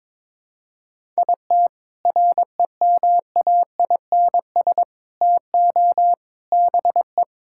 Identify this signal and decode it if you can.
Morse code, heard as a keyed tone, about 22 wpm, 700 Hz. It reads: IT REMAINS TO BE
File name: Sentences with 7-Word Phrases - 22wpm